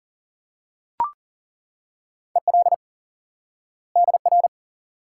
Morse code, heard as a keyed tone, about 40 wpm, 700 Hz. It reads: EL DR
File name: course / instant-qso-element-course-3 / IQE Course 3 - Lesson 89 - YAGI - Two Elements - 40wpm